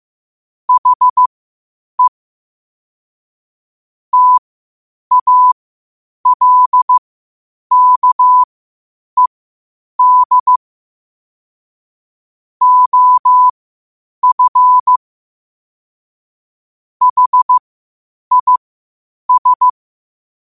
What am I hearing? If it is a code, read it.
Morse code, heard as a keyed tone, about 15 wpm, spaced as for 8 wpm, 1000 Hz. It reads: HE TALKED OF HIS
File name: qotdfiles / 8wpm